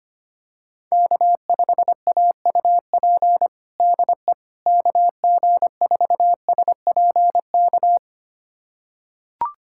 Morse code, heard as a keyed tone, about 25 wpm, 700 Hz. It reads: K5AUP DE KG4SPK
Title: USA Callsign DE Callsign 25wpm